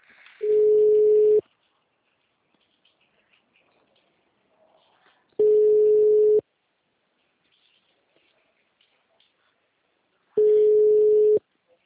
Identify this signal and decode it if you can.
Ringback tone (Europe, 425 Hz), with speech.